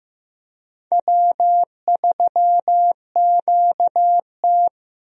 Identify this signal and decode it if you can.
Morse code, heard as a keyed tone, about 16 wpm, 700 Hz. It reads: W3QT
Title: USA Call Signs - 4 Characters - Repeated 2x - 15wpm